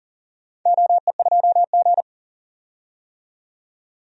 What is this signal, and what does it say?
Morse code, heard as a keyed tone, about 40 wpm, 700 Hz. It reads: OE2G
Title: DXCC-100 Call Signs - 4 Characters - 40wpm